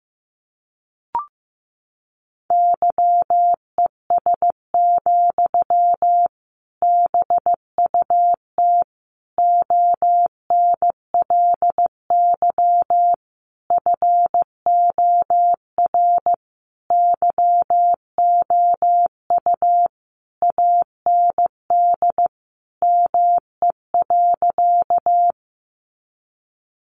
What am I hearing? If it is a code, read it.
Morse code, heard as a keyed tone, about 15 wpm, 700 Hz. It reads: YES, BUT ONLY FOR YOU AND ME.
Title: Sentences from Top 500 Words 15wpm